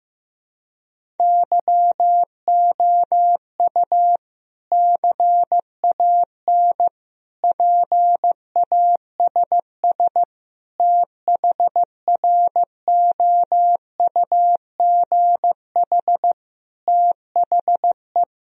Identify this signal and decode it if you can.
Morse code, heard as a keyed tone, about 15 wpm, 700 Hz. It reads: YOU CAN PASS THROUGH THE
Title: Sentences from Top 1000 Words - Review Part 1 - 15wpm